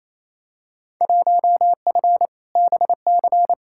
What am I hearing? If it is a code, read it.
Morse code, heard as a keyed tone, about 28 wpm, 700 Hz. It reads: 1F BC
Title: Field Day Exchanges Encore - 28wpm